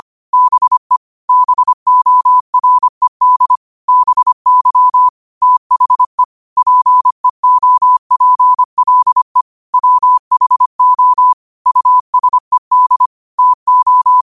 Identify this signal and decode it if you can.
Morse code, heard as a keyed tone, about 25 wpm, 1000 Hz. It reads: BE BORED BY THE PEOPLE WHO USED TO